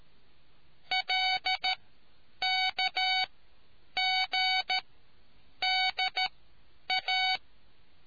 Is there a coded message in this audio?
Yes: Morse code from 0.9 to 7.4 s, heard as a keyed tone, about 13 wpm, spaced as for 8 wpm, 750 Hz. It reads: LKGDA